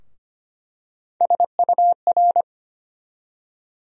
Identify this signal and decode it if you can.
Morse code, heard as a keyed tone, about 25 wpm, 700 Hz. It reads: SUR
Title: USA Call Sign Postfixes - 3 Letters 25wpm